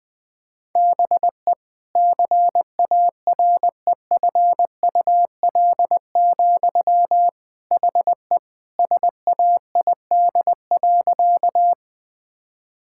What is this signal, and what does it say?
Morse code, heard as a keyed tone, about 20 wpm, 700 Hz. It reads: BE CAREFUL, HE SAID.
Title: Sentences from Top 1600 Words 20wpm